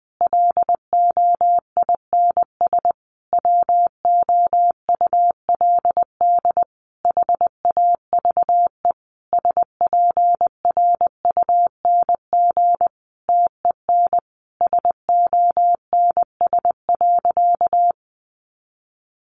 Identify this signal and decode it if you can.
Morse code, heard as a keyed tone, about 20 wpm, 700 Hz. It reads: LOINS WOULD HAVE SPRUNG TEN SONS.
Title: A Sound Of Thunder 20wpm